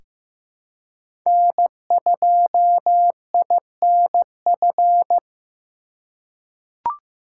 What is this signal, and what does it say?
Morse code, heard as a keyed tone, about 15 wpm, 700 Hz. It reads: N2INF
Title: USA Call Signs - 5 Characters - 3x Character Spacing - 15wpm